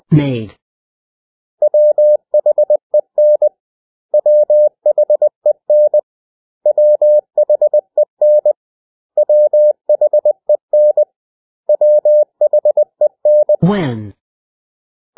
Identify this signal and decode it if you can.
Morse code, heard as a keyed tone, about 20 wpm, 600 Hz. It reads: WHEN WHEN WHEN WHEN WHEN
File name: cw / 100 Common English Words